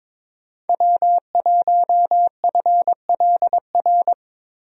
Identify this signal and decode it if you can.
Morse code, heard as a keyed tone, about 22 wpm, 700 Hz. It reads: W1FLR
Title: USA Call Signs - 5 Characters - 22wpm